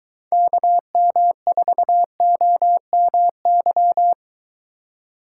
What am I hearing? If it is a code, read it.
Morse code, heard as a keyed tone, about 23 wpm, 700 Hz. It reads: KM4OMY